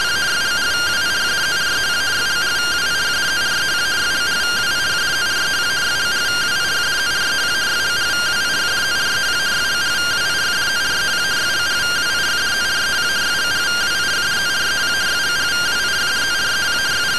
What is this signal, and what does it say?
Two-tone teleprinter signal, about 170 Hz shift.